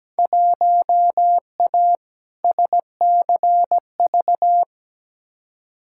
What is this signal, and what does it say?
Morse code, heard as a keyed tone, about 17 wpm, 700 Hz. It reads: 1A SCV
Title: Field Day Exchanges - 17wpm